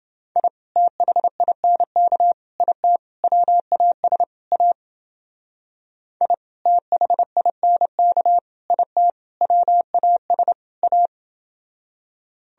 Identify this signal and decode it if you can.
Morse code, heard as a keyed tone, about 28 wpm, 700 Hz. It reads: I THINK IT WAS A I THINK IT WAS A